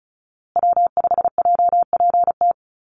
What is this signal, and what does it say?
Morse code, heard as a keyed tone, about 34 wpm, 700 Hz. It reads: W5JPT
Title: USA Call Signs Repeated 2x 35wpm